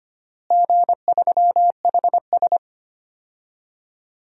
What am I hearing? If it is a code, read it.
Morse code, heard as a keyed tone, about 25 wpm, 700 Hz. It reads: G3HS
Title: DXCC-100 Call Signs - 4 Characters - Repeated 3x - 25wpm